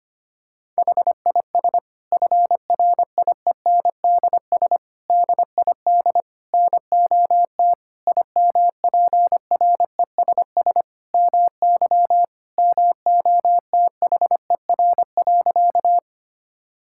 Morse code, heard as a keyed tone, about 25 wpm, 700 Hz. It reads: HIS FRIENDS DID NOT IMPRESS MY MOTHER.
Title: Sentences from Top 1600 Words 25wpm